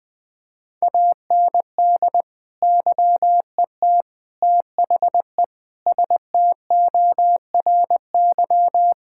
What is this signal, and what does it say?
Morse code, heard as a keyed tone, about 20 wpm, 700 Hz. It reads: AND YET THE STORY
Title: Sentences from Top 1000 Words - Review Part 6 - 20wpm